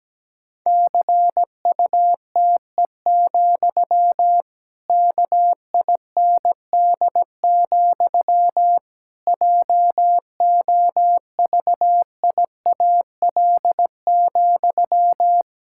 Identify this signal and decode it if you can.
Morse code, heard as a keyed tone, about 17 wpm, 700 Hz. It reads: CUTE, KIND, JOVIAL,
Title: Pangram Challenge v2 - 17wpm